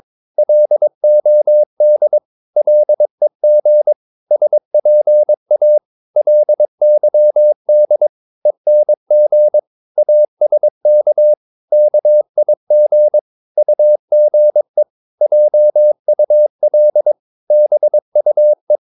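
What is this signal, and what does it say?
Morse code, heard as a keyed tone, about 22 wpm, 600 Hz. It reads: LOD LEG SPA LYD ENG ASK KIG UGE JUL BUE
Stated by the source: Morse practice of short Danish words at 22 wpm, from CW med Gnister.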